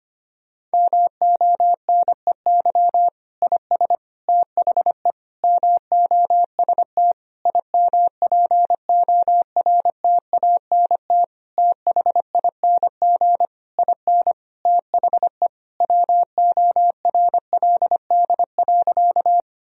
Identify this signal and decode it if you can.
Morse code, heard as a keyed tone, about 25 wpm, 700 Hz. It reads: MONEY IS THE MOST IMPORTANT THING IN THE WORLD.
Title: Sentences from Top 1000 Words - Review Part 1 - 25wpm